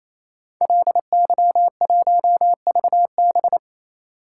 Morse code, heard as a keyed tone, about 27 wpm, 700 Hz. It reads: LY1VB
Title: Code Groups - 5 Letter:Numbers - 28wpm